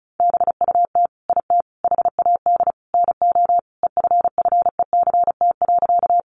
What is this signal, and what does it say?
Morse code, heard as a keyed tone, about 34 wpm, 700 Hz. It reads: BUT IT HAD NO EFFECT.
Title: Sentences from Top 1000 Words 35wpm